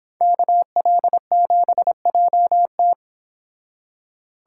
Morse code, heard as a keyed tone, about 29 wpm, 700 Hz. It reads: KL7JT